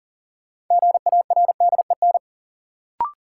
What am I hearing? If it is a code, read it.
Morse code, heard as a keyed tone, about 40 wpm, 700 Hz. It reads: GARDEN
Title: Sentences from Top 700 Words - New Word Drill - 40wpm